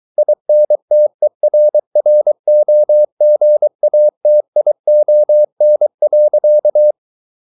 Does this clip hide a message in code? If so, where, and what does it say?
Yes: Morse code from 0.2 to 6.9 s, heard as a keyed tone, about 23 wpm, 600 Hz. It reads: INTERROGATION.